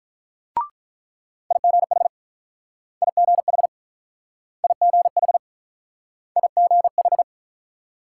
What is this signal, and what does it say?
Morse code, heard as a keyed tone, about 47 wpm, 700 Hz. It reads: IGH IGH IGH IGH